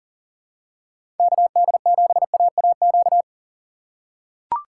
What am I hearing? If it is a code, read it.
Morse code, heard as a keyed tone, about 40 wpm, 700 Hz. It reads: KD7AAQ
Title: USA Call Signs - 6 Characters - 3x Character Spacing - 40wpm